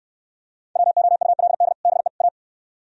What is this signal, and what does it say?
Morse code, heard as a keyed tone, about 64 wpm, 700 Hz. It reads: WOULD BE A